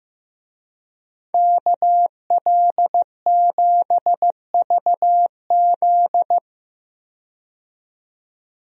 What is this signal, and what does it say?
Morse code, heard as a keyed tone, about 15 wpm, 700 Hz. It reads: KL7VZ
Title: DXCC-100 Call Signs - 5 Characters - 15wpm